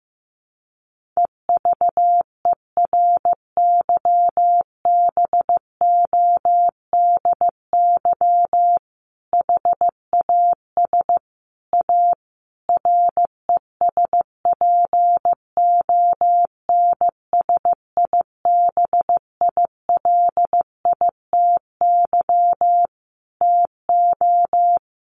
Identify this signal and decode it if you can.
Morse code, heard as a keyed tone, about 15 wpm, 700 Hz. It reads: EVERYBODY HAS A RESPONSIBILITY TO
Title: Sentences from Top 1400 Words 15wpm